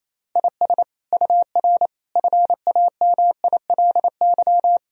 Morse code, heard as a keyed tone, about 28 wpm, 700 Hz. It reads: IS UR FAMILY
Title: Sentences from Top 400 Words and Abbreviations 28wpm